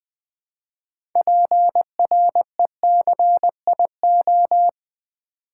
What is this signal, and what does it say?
Morse code, heard as a keyed tone, about 20 wpm, 700 Hz. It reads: PRECIO
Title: Build A Word - 3 Letters At A Time - 8-Letter Words - 20wpm